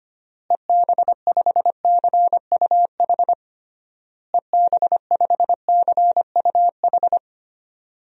Morse code, heard as a keyed tone, about 25 wpm, 700 Hz. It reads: EB5CUH EB5CUH
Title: DXCC-100 Call Signs - 6 Characters - Repeated 3x - 25wpm